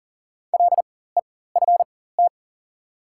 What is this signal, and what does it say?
Morse code, heard as a keyed tone, about 37 wpm, spaced as for 18 wpm, 700 Hz. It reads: LEFT
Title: 4x Character Spacing - Top 300 Words - 40wpm